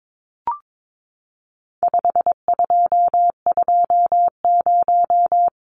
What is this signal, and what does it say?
Morse code, heard as a keyed tone, about 22 wpm, 700 Hz. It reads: 5220
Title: Numbers - 4-Digit 22wpm